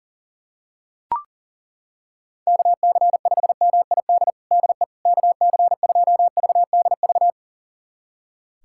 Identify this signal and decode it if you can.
Morse code, heard as a keyed tone, about 40 wpm, 700 Hz. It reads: KC5MID DE KC2VDV
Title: USA Callsign DE Callsign - 6 Characters - 40wpm